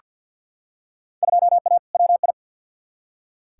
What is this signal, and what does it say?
Morse code, heard as a keyed tone, about 46 wpm, 700 Hz. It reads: 2A WI
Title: Field Day Exchanges - 50wpm